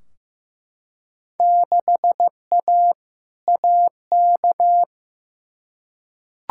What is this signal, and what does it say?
Morse code, heard as a keyed tone, about 15 wpm, 700 Hz. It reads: 6A AK
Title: Field Day Exchanges Encore - 15wpm